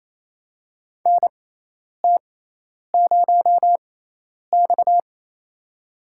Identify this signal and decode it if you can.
Morse code, heard as a keyed tone, about 28 wpm, spaced as for 10 wpm, 700 Hz. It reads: NT0X
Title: USA Call Signs - 6x Character Spacing - 28wpm